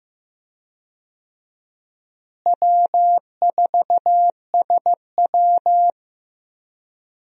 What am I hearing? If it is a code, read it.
Morse code, heard as a keyed tone, about 15 wpm, 700 Hz. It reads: W4SW
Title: USA Call Signs - 4 Characters - Repeated 3x - 15wpm